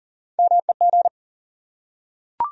Morse code, heard as a keyed tone, about 40 wpm, 700 Hz. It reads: MEG
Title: USA Common Names - Max 3 Letters - 40wpm